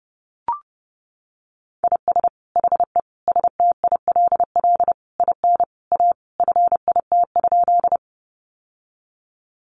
Morse code, heard as a keyed tone, about 30 wpm, 700 Hz. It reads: IS HE STILL IN A FIT?
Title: Sentences from Top 800 Words 30wpm